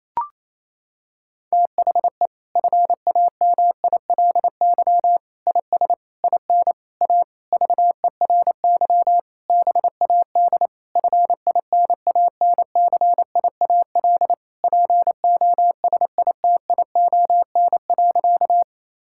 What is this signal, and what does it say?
Morse code, heard as a keyed tone, about 28 wpm, 700 Hz. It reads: THE FAMILY IS IN A VERY BAD FINANCIAL POSITION.
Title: Sentences from Top 1000 Words - Review Part 5 - 28wpm